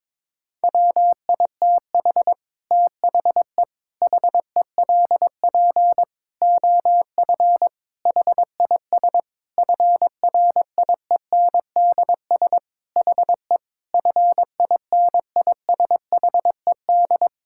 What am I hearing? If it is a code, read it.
Morse code, heard as a keyed tone, about 22 wpm, 700 Hz. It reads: WITH THE HELP OF HIS FRIENDS HE FINISHED
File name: practice / 22wpm / Sentences with 4-Word Phrases - 22wpm